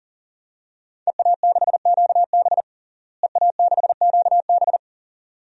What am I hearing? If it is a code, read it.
Morse code, heard as a keyed tone, about 40 wpm, 700 Hz. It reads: EA6QB EA6QB
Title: DXCC-100 Call Signs - 5 Characters - Repeated 2x - 40wpm